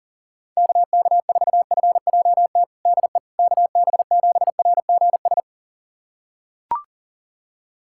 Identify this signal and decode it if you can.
Morse code, heard as a keyed tone, about 40 wpm, 700 Hz. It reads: KK4FJT DE KB7RGS
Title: USA Callsign DE Callsign - 6 Characters - 40wpm